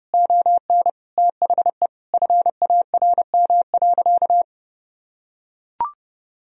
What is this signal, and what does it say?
Morse code, heard as a keyed tone, about 30 wpm, 700 Hz. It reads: ON THE FARM.
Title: Sentences from Top 1000 Words - Review Part 3 - 30wpm